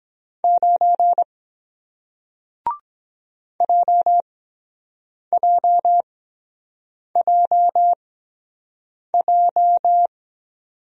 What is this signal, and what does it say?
Morse code, heard as a keyed tone, about 23 wpm, spaced as for 7 wpm, 700 Hz. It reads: 9 JJJJ